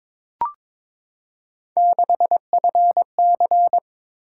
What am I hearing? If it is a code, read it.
Morse code, heard as a keyed tone, about 23 wpm, 700 Hz. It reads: 6FC
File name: practice / 22wpm / Three Letter-Number Combinations - 22wpm